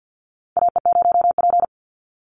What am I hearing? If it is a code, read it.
Morse code, heard as a keyed tone, about 49 wpm, 700 Hz. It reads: AE0P
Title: Call Signs Repeated 3x - 50wpm